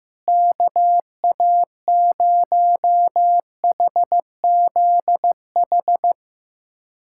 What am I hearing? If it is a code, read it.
Morse code, heard as a keyed tone, about 15 wpm, 700 Hz. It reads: KA0HZH
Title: USA Callsign DE Callsign 15wpm